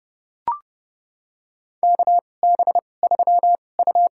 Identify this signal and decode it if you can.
Morse code, heard as a keyed tone, about 30 wpm, spaced as for 22 wpm, 700 Hz. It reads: KB3U
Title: USA Call Signs - 2x Character Spacing - 30wpm